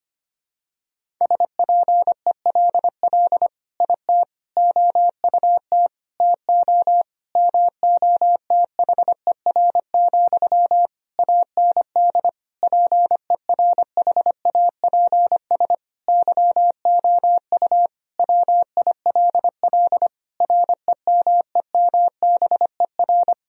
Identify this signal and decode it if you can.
Morse code, heard as a keyed tone, about 25 wpm, 700 Hz. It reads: SPELL IT OUT TO MOTHER, AND PERHAPS YOU WILL REMEMBER
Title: Sentences from Top 1000 Words - Review Part 1 - 25wpm